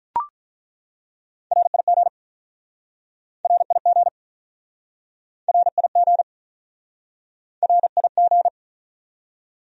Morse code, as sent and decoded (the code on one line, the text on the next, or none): .-. .. --. / .-. .. --. / .-. .. --. / .-. .. --.
RIG RIG RIG RIG